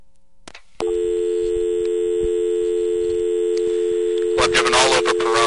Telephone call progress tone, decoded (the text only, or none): dial tone